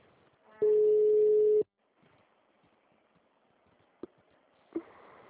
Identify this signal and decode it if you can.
Ringback tone (Europe, 425 Hz).